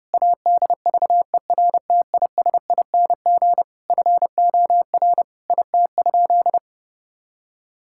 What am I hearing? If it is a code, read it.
Morse code, heard as a keyed tone, about 30 wpm, 700 Hz. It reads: ADVERTISING FOR IT?
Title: Sentences from Top 1000 Words - Review Part 6 - 30wpm